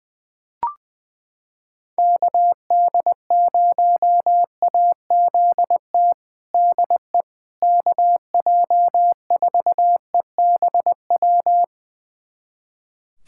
Morse code, heard as a keyed tone, about 20 wpm, 700 Hz. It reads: KD0AZT DE KJ4EBW
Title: USA Callsign DE Callsign 20wpm